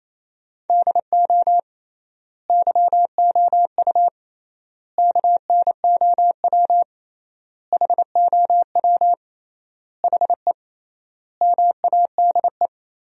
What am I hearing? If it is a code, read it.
Morse code, heard as a keyed tone, about 28 wpm, 700 Hz. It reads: DO YOU KNOW HOW HE MADE
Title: 3x Word-Spacing - Sentences from Top 300 Words - 28wpm